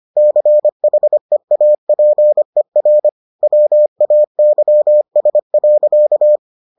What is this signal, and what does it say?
Morse code, heard as a keyed tone, about 25 wpm, 600 Hz. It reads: CHEAPER WAYS.